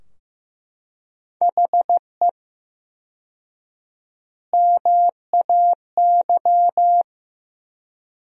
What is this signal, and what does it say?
Morse code, heard as a keyed tone, about 15 wpm, 700 Hz. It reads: HE MAY
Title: 4x Word-Spacing - Sentences from Top 300 Words - 15wpm